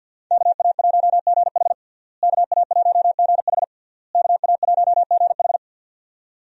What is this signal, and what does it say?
Morse code, heard as a keyed tone, about 50 wpm, 700 Hz. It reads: KA1GH KA1GH KA1GH
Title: USA Call Signs - 5 Characters - Repeated 3x - 50wpm